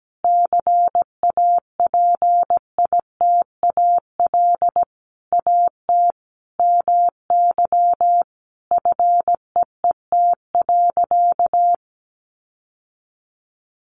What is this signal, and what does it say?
Morse code, heard as a keyed tone, about 17 wpm, 700 Hz. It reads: CAPITAL AT MY FEET.
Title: Sentences from Top 1000 Words 17wpm